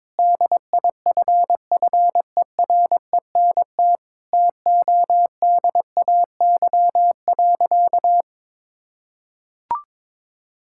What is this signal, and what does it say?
Morse code, heard as a keyed tone, about 22 wpm, 700 Hz. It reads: DIFFERENT TODAY.
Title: Sentences from Top 1600 Words 22wpm